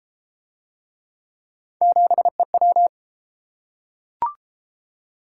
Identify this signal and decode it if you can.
Morse code, heard as a keyed tone, about 30 wpm, 700 Hz. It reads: 7EW